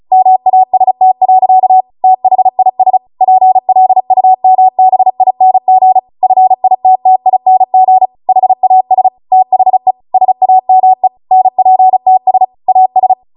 Morse code, heard as a keyed tone, about 36 wpm, 750 Hz. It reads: MAST. THIS PLUMBING FITTING HAS THE SAME NPTS AS